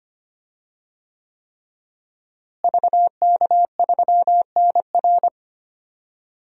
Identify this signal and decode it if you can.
Morse code, heard as a keyed tone, about 25 wpm, 700 Hz. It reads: VK3NR
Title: DXCC-100 Call Signs - 5 Characters - Repeated 3x - 25wpm